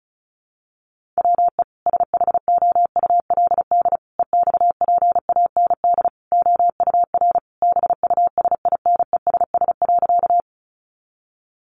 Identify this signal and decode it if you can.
Morse code, heard as a keyed tone, about 34 wpm, 700 Hz. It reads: WE SHOULD EXPAND OUR BUSINESS.
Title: Sentences from Top 1400 Words 35wpm